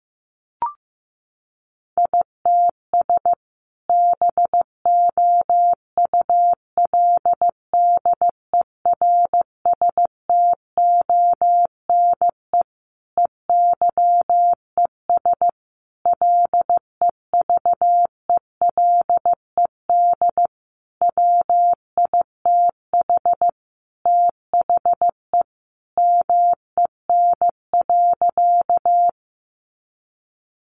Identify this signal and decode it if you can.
Morse code, heard as a keyed tone, about 15 wpm, 700 Hz. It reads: ITS BOULDERSTONE EYES LEVELED WITH THE MEN.
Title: A Sound Of Thunder 15wpm